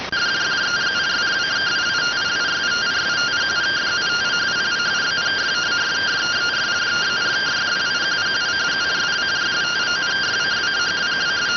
Two-tone teleprinter signal, about 170 Hz shift.